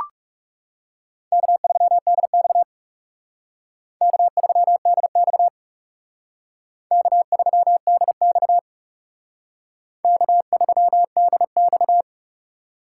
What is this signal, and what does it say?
Morse code, heard as a keyed tone, about 37 wpm, 700 Hz. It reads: K3DX K3DX K3DX K3DX
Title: Speed Racing - USA 1x2 Call Signs 30wpm